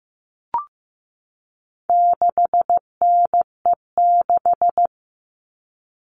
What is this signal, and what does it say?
Morse code, heard as a keyed tone, about 15 wpm, 700 Hz. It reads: 6NE6
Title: All Cut Numbers - 4-Digit 15wpm